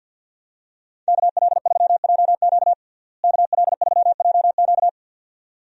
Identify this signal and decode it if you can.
Morse code, heard as a keyed tone, about 51 wpm, 700 Hz. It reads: KL3JQ KL3JQ
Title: USA Call Signs - 5 Characters - Repeated 2x - 50wpm